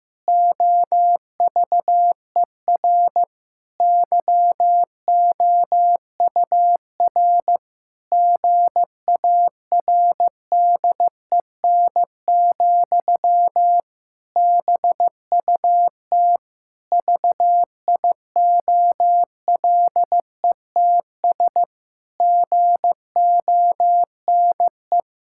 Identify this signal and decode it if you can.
Morse code, heard as a keyed tone, about 15 wpm, 700 Hz. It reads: OVER YOUR GARDEN, BUT VIOLETS GONE
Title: Pangram Challenge v2 - 15wpm